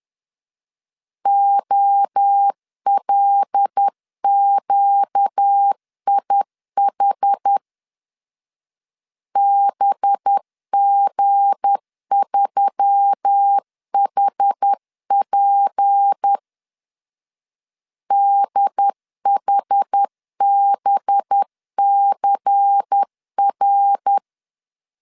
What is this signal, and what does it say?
Morse code, heard as a keyed tone, about 10 wpm, 800 Hz. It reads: OLQIH BG3HP DHBCR
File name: LEZIONI / 7 GF9Z / 7 G 10wpm 2501